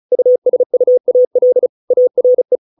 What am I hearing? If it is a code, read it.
Morse code, heard as a keyed tone, about 34 wpm, 500 Hz. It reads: USUAL ARE